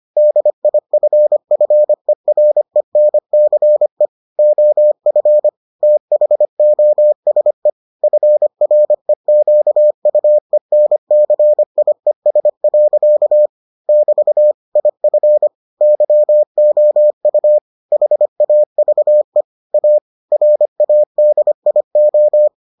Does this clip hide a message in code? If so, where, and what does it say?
Yes: Morse code from 0.2 to 22.5 s, heard as a keyed tone, about 25 wpm, 600 Hz. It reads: DIFFERENCE OF THOSE FREQUENCIES. <BT> IF YOU HAVE A RADIO